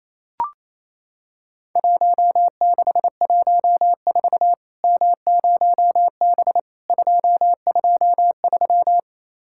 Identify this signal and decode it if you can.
Morse code, heard as a keyed tone, about 28 wpm, 700 Hz. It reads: 1614 M0B 223
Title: Worked All Europe DX Contest - QTC Exchange - 28wpm